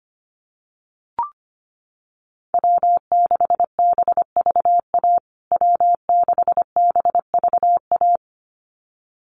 Morse code, heard as a keyed tone, about 25 wpm, 700 Hz. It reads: W6BVA W6BVA
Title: USA Call Signs Repeated 2x 25wpm